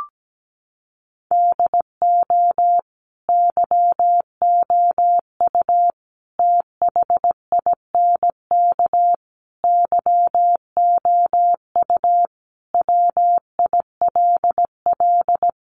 Morse code, heard as a keyed tone, about 17 wpm, 700 Hz. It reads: DO YOU THINK YOU WILL
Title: Sentences from Top 1300 Words 17wpm